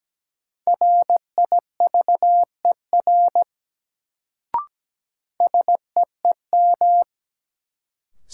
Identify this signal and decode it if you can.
Morse code, heard as a keyed tone, about 17 wpm, 700 Hz. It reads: RIVER SEEM
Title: Sentences from Top 300 Words - New Word Drill - 17wpm